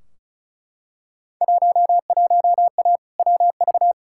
Morse code, heard as a keyed tone, about 36 wpm, 700 Hz. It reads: 11A WV
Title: Field Day Exchanges Encore - 35wpm